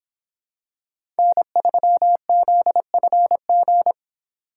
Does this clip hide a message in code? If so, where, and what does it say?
Yes: Morse code from 1.2 to 3.9 s, heard as a keyed tone, about 26 wpm, 700 Hz. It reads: N3ZFG